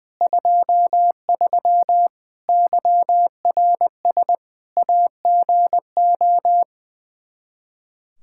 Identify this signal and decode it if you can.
Morse code, heard as a keyed tone, about 21 wpm, 700 Hz. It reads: 23 YRS AGO
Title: K1USN SST QSO Practice 20wpm